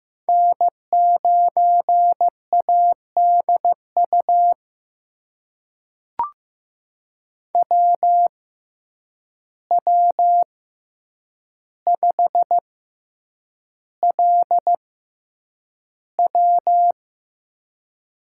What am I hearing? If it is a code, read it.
Morse code, heard as a keyed tone, about 15 wpm, spaced as for 8 wpm, 700 Hz. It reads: N9ADU WW5LW
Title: USA Call Signs - 5 Characters - 6x Character Spacing - 15wpm